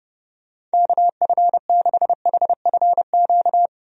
Morse code, heard as a keyed tone, about 32 wpm, 700 Hz. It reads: KF6HFQ
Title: USA Call Signs - 6 Characters - 30wpm